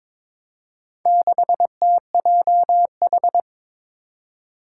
Morse code, heard as a keyed tone, about 22 wpm, 700 Hz. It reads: 6TJH
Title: Four Challenging Letter-Number Combinations 22wpm